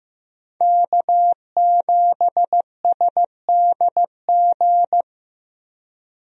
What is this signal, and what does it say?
Morse code, heard as a keyed tone, about 15 wpm, 700 Hz. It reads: K7SDG
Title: USA Call Signs - 5 Characters - Repeated 3x - 15wpm